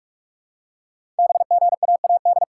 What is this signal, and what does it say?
Morse code, heard as a keyed tone, about 44 wpm, 700 Hz. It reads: BGAAD
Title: Code Groups - 5 Letters - 45wpm